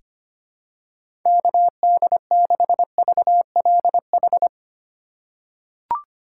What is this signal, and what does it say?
Morse code, heard as a keyed tone, about 25 wpm, 700 Hz. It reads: KD6VLH
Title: USA Call Signs - 2x Character Spacing - 25wpm